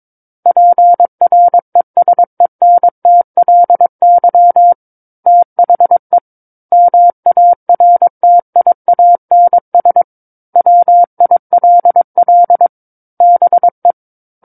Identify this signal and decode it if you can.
Morse code, heard as a keyed tone, about 22 wpm, 700 Hz. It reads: PRESENTLY THE MARTIANS WILL BE
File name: Book One / 13-Chapter 13 - 22 WPM